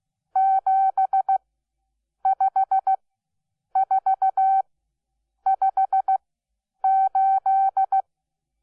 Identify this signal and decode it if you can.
Morse code, heard as a keyed tone, about 16 wpm, spaced as for 8 wpm, 770 Hz. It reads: 75458
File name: Onlinelehrgang / morsen / DJ4UF Morsen12